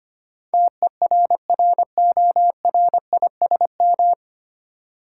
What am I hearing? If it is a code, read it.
Morse code, heard as a keyed tone, about 25 wpm, 700 Hz. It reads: TERRORISM
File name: Nine Letter Words - 25wpm